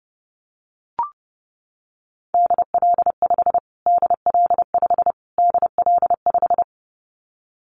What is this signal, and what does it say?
Morse code, heard as a keyed tone, about 30 wpm, 700 Hz. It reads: DL5 DL5 DL5
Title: Callsign Prefixes for Top 25 DXCC - Repeated 3x - 30wpm